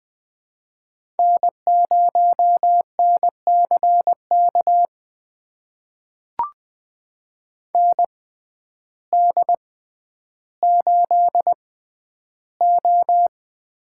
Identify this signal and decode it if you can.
Morse code, heard as a keyed tone, about 21 wpm, spaced as for 11 wpm, 700 Hz. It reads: N0NCK ND8O
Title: USA Call Signs - 6x Character Spacing - 20wpm